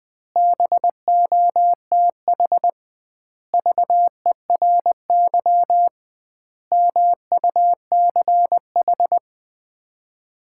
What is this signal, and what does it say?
Morse code, heard as a keyed tone, about 19 wpm, 700 Hz. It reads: BOTH VERY MUCH
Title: 2x Word-Spacing - Sets of 5 Words - 20wpm